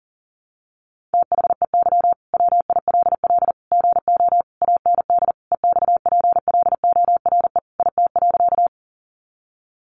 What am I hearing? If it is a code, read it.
Morse code, heard as a keyed tone, about 37 wpm, 700 Hz. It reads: THEY WILL GO AND EXPLORE IT.
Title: Sentences from Top 1100 Words 40wpm